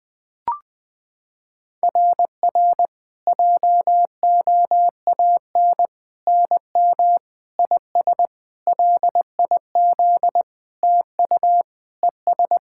Morse code, heard as a keyed tone, about 20 wpm, 690 Hz. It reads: RR JOAN NM IS LIZ TU ES
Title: SOTA QSO Practice 20wpm